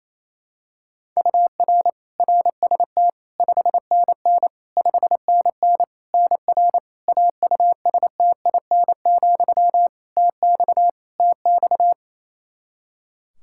Morse code, heard as a keyed tone, about 28 wpm, 700 Hz. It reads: UR RST 5NN 5NN NR AUSTIN, TX TX
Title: Practice QSO - Exchange 1 - Beginning - 28wpm